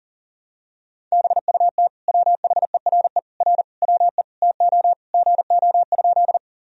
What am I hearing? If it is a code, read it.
Morse code, heard as a keyed tone, about 40 wpm, 700 Hz. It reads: BUT WHERE R WE TO GO?
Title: Sentences from Top 200 Words and Abbreviations 40wpm